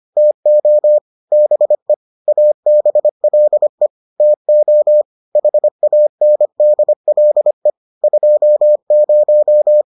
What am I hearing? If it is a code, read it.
Morse code, heard as a keyed tone, about 25 wpm, 600 Hz. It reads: TO BE ABLE TO HANDLE 20